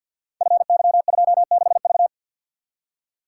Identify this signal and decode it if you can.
Morse code, heard as a keyed tone, about 52 wpm, 700 Hz. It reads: FY26V
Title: Code Groups - 5 Letter:Numbers - 50wpm